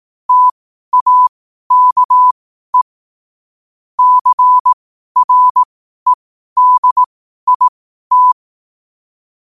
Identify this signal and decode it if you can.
Morse code, heard as a keyed tone, about 18 wpm, spaced as for 12 wpm, 1000 Hz. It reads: TAKE CREDIT
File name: qotdfiles / 12wpm